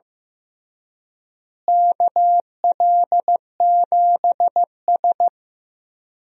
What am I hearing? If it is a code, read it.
Morse code, heard as a keyed tone, about 16 wpm, 700 Hz. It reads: KL7S
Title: DXCC-100 Call Signs - 4 Characters - Repeated 3x - 15wpm